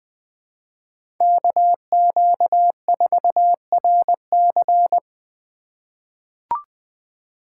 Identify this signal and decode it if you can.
Morse code, heard as a keyed tone, about 20 wpm, 700 Hz. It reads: KQ4RC
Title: USA Call Signs - 5 Characters - 3x Character Spacing - 20wpm